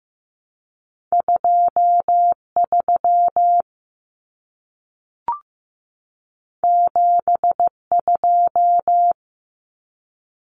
Morse code, heard as a keyed tone, about 15 wpm, 700 Hz. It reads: 23 72
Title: Common Cut Numbers - 2-Digit 15wpm